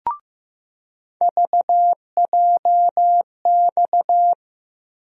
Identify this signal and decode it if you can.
Morse code, heard as a keyed tone, about 15 wpm, 700 Hz. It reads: VJX
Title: USA Call Sign Postfixes - 3 Letters 15wpm